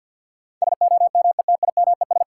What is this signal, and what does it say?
Morse code, heard as a keyed tone, about 52 wpm, 700 Hz. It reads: SOMETIMES
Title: Sentences from Top 600 Words - New Word Drill - 50wpm